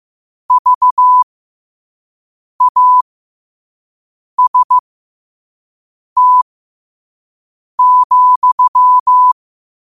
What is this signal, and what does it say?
Morse code, heard as a keyed tone, about 15 wpm, spaced as for 5 wpm, 1000 Hz. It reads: VAST,